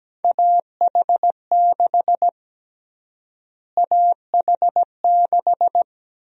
Morse code, heard as a keyed tone, about 17 wpm, 700 Hz. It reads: AH6 AH6
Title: USA Call Sign Prefixes - Repeated 2x - 17wpm